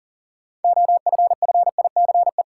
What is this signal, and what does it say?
Morse code, heard as a keyed tone, about 39 wpm, 700 Hz. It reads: OFFICE
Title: Sentences from Top 1000 Words - New Word Drill - 40wpm